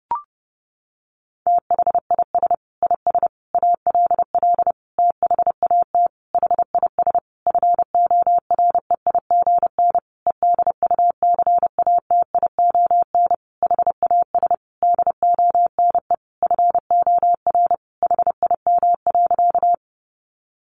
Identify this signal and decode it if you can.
Morse code, heard as a keyed tone, about 30 wpm, 700 Hz. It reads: THIS IS ALL THAT HIS FOREIGN EDUCATION HAS DONE FOR HIM.
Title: Sentences from Top 1400 Words 30wpm